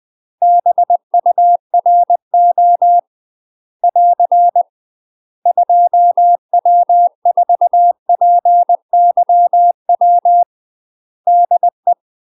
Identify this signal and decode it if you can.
Morse code, heard as a keyed tone, about 20 wpm, 700 Hz. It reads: BURO <AR> 2W4PYW DE
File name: UK MASTER QSOs TRAINING / 20WPM / UK MASTER QSOs 0052